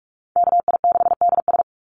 Morse code, heard as a keyed tone, about 44 wpm, 700 Hz. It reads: KI6DS
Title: NA SOTA Chaser Callsigns 45wpm